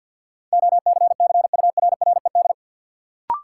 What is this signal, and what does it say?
Morse code, heard as a keyed tone, about 51 wpm, 700 Hz. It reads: OCCURRED
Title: Sets of 5 Words Encore - New Word Drill - 50wpm